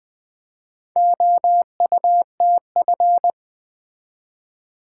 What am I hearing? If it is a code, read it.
Morse code, heard as a keyed tone, about 20 wpm, 700 Hz. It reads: OUTF
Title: Build A Word - 2 Letters At A Time - 6-Letter Words - 20wpm